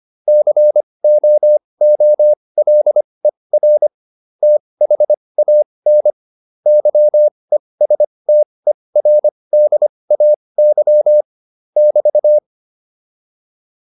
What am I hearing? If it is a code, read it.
Morse code, heard as a keyed tone, about 25 wpm, spaced as for 20 wpm, 600 Hz. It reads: COOLER THAN YESTERDAY <BT>